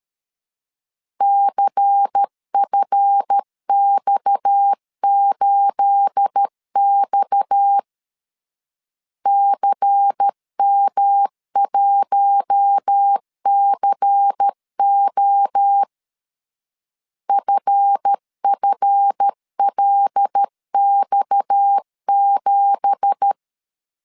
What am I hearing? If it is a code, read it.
Morse code, heard as a keyed tone, about 13 wpm, 800 Hz. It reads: CFX8X CM1CO FFLX7
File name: CW / DET / 1 9 1500C S A 12wpm 3800